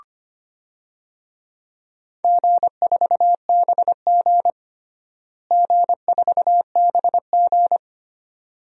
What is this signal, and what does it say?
Morse code, heard as a keyed tone, about 25 wpm, 700 Hz. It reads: G4BG G4BG
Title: DXCC-100 Call Signs - 4 Characters - Repeated 2x - 25wpm